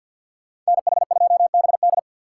Morse code, heard as a keyed tone, about 53 wpm, 700 Hz. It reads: NH2BD
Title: USA Call Signs - 5 Characters - 6x Character Spacing - 50wpm